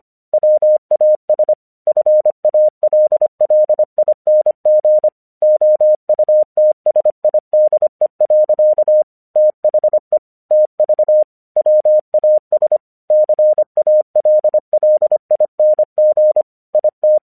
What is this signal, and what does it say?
Morse code, heard as a keyed tone, about 25 wpm, 610 Hz. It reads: WAS FALLING OUTSIDE. THE TV WAS CALLING IT
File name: 1 English 25wpm / AQC18